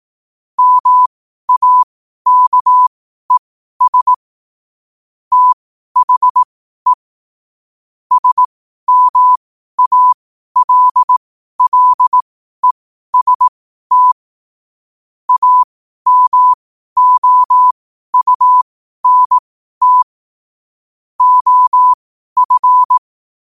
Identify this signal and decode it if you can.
Morse code, heard as a keyed tone, about 18 wpm, spaced as for 12 wpm, 1000 Hz. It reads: MAKES THE SMALLEST AMOUNT OF
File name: qotdfiles / 12wpm